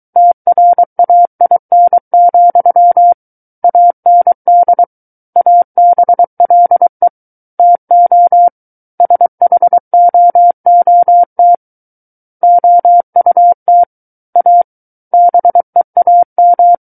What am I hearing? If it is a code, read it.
Morse code, heard as a keyed tone, about 23 wpm, 700 Hz. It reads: TRAIN, AND ABLE TO SHOOT OUT A BEAM
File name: Book One / 14-Chapter 14 - 23 WPM